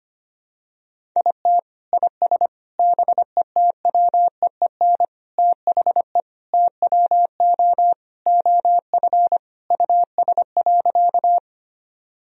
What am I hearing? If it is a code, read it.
Morse code, heard as a keyed tone, about 25 wpm, 700 Hz. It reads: IT IS BETWEEN THE TWO OF US.
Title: Sentences from Top 1000 Words - Review Part 1 - 25wpm